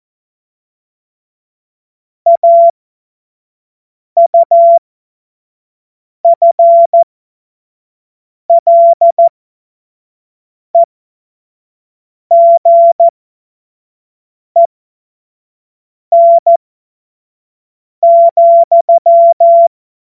Morse code, heard as a keyed tone, about 14 wpm, spaced as for 5 wpm, 680 Hz. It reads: AUFLEGEN,
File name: Morsekurs / Witze / Witz 11. Juli 2023 25 F70